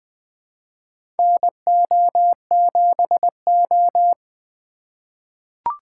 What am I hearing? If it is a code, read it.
Morse code, heard as a keyed tone, about 21 wpm, 700 Hz. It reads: NO7O
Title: USA Call Signs - 4x Character Spacing - 20wpm